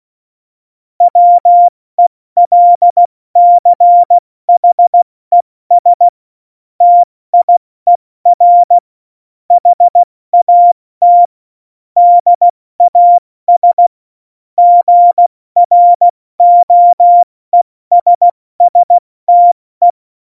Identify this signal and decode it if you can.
Morse code, heard as a keyed tone, about 16 wpm, 700 Hz. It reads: WELCHES TIER HAT DAS GROESSTE
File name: Morsekurs / Witze / Witz 14. Mai 2019, 70 F80